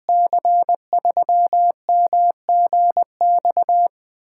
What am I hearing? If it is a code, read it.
Morse code, heard as a keyed tone, about 20 wpm, 700 Hz. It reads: C3MGX
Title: USA Call Signs - 6 Characters - 8x Character Spacing - 20wpm